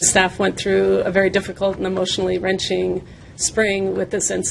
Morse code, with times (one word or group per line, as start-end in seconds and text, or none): none